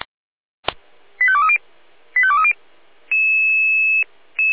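Five-tone selective call (ZVEI), heard as quick runs of five tones, tones of about 70 ms. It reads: 85230 85230
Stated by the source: ZVEI five-tone call 85232 received by SDR.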